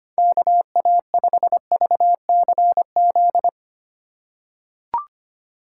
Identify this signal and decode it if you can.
Morse code, heard as a keyed tone, about 25 wpm, 700 Hz. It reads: KA5VCZ
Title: USA Call Signs - 2x Character Spacing - 25wpm